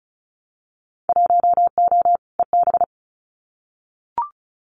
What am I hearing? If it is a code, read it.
Morse code, heard as a keyed tone, about 34 wpm, 700 Hz. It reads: 1O EB
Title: Winter Field Day Exchanges 35wpm